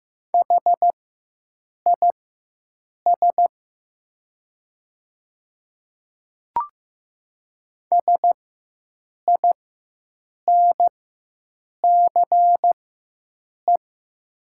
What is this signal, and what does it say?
Morse code, heard as a keyed tone, about 15 wpm, spaced as for 7 wpm, 700 Hz. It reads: HIS SINCE
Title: 4x Character Spacing - Top 300 Words - 15wpm